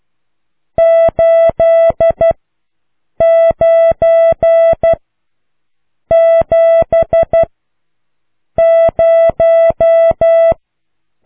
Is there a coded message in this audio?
Yes: Morse code from 0.8 to 10.5 s, heard as a keyed tone, about 12 wpm, 660 Hz. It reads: 8 9 7 0